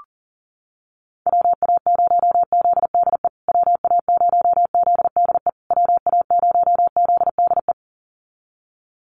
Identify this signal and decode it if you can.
Morse code, heard as a keyed tone, about 40 wpm, 700 Hz. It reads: WA0ZDE WA0ZDE WA0ZDE
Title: USA Call Signs Repeated 3x 40wpm